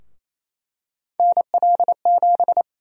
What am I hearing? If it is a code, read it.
Morse code, heard as a keyed tone, about 28 wpm, 700 Hz. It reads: NL7
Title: USA Call Sign Prefixes - Repeated 2x - 28wpm